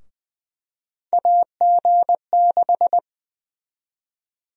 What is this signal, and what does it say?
Morse code, heard as a keyed tone, about 20 wpm, 700 Hz. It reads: AG6
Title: USA Call Sign Prefixes - 20wpm